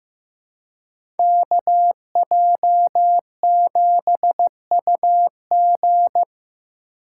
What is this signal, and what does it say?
Morse code, heard as a keyed tone, about 16 wpm, 700 Hz. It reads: KJ7UG
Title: USA Call Signs - 3x Character Spacing - 15wpm